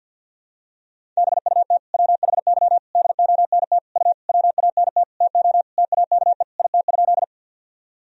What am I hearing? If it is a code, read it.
Morse code, heard as a keyed tone, about 50 wpm, 700 Hz. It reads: BUT WHY DONT U WANT TO TAKE IT?
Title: Sentences from Top 300 Words and Abbreviations 50wpm